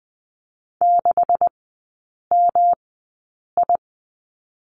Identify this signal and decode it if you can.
Morse code, heard as a keyed tone, about 20 wpm, spaced as for 8 wpm, 690 Hz. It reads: 6MI
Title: 145 Lesson 24e - Letter y - Three Characters 20wpm